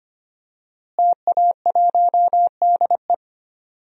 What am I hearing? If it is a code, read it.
Morse code, heard as a keyed tone, about 25 wpm, 700 Hz. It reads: TA1DE
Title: DXCC-100 Call Signs - 5 Characters - Repeated 2x - 25wpm